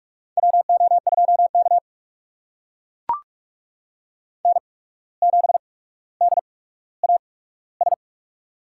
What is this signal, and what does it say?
Morse code, heard as a keyed tone, about 44 wpm, spaced as for 18 wpm, 700 Hz. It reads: WO2K N7DAS
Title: USA Call Signs - 8x Character Spacing - 45wpm